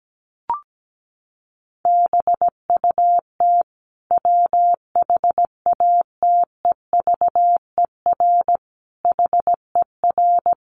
Morse code, heard as a keyed tone, about 17 wpm, 700 Hz. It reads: BUT WHATEVER HER
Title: Sentences from Top 1100 Words 17wpm